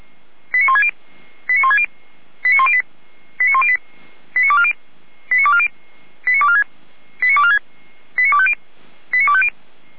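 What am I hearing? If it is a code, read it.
Five-tone selective call (ZVEI), heard as quick runs of five tones, tones of about 70 ms. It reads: 89160 89160 89108 89108 89240 89240 89246 89246 89250 89250
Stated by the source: ZVEI five-tone selective call received with an SDR.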